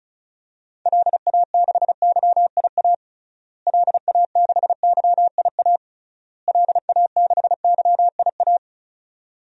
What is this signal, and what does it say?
Morse code, heard as a keyed tone, about 35 wpm, 700 Hz. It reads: LA6YIA LA6YIA LA6YIA
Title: DXCC-100 Call Signs - 6 Characters - Repeated 3x - 35wpm